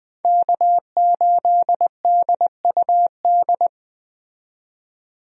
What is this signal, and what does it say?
Morse code, heard as a keyed tone, about 20 wpm, 700 Hz. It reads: K8DUD
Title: USA Call Signs - 5 Characters - 20wpm